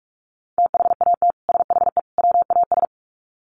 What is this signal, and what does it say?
Morse code, heard as a keyed tone, about 44 wpm, 700 Hz. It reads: THAT SHE WAS
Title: Sets of 3 Words 45wpm